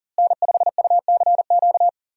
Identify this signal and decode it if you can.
Morse code, heard as a keyed tone, about 40 wpm, 700 Hz. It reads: N5UCQ
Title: USA Call Signs - 5 Characters - Repeated 3x - 40wpm